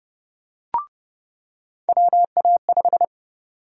Morse code, heard as a keyed tone, about 29 wpm, 700 Hz. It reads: WA5
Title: USA Call Sign Prefixes - 30wpm